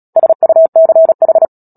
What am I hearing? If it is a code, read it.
Morse code, heard as a keyed tone, about 37 wpm, 650 Hz. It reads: SUCH
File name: Book Two / 10-Chapter 10 - Epilogue - 36 WPM